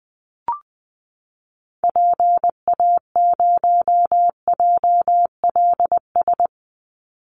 Morse code, heard as a keyed tone, about 20 wpm, 700 Hz. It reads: PA0JLS
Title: Call Signs Repeated 3x - 20wpm